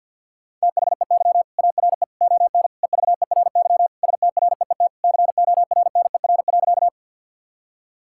Morse code, heard as a keyed tone, about 50 wpm, 700 Hz. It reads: THEY ARE ON EVERY STREET CORNER.
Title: Sentences from Top 1600 Words 50wpm